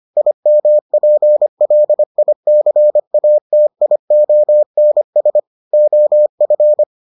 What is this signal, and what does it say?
Morse code, heard as a keyed tone, about 25 wpm, 600 Hz. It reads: IMPLICATIONS OF